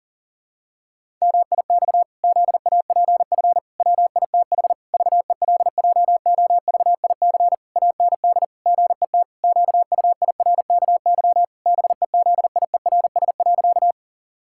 Morse code, heard as a keyed tone, about 40 wpm, 700 Hz. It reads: MIX ZAPF WITH VELJOVIC AND GET QUIRKY BEZIERS.
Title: Pangram Challenge v2 - 40wpm